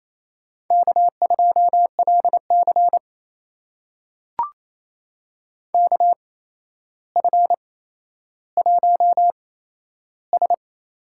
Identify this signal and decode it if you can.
Morse code, heard as a keyed tone, about 28 wpm, spaced as for 12 wpm, 700 Hz. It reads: K2LC KF1S
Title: USA Call Signs - 4 Characters - 8x Character Spacing - 28wpm